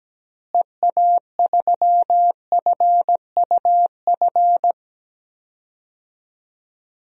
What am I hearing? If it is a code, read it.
Morse code, heard as a keyed tone, about 17 wpm, 700 Hz. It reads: EA3FUF
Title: DXCC-100 Call Signs - 6 Characters - Repeated 3x - 17wpm